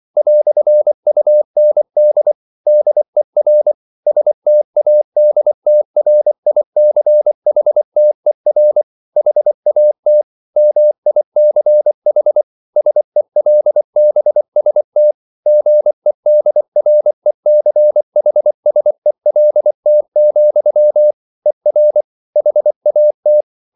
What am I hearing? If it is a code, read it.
Morse code, heard as a keyed tone, about 24 wpm, 600 Hz. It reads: "UND DER STADTRICHTER HAT MICH SELBST GEDRECHSELT, ER HAT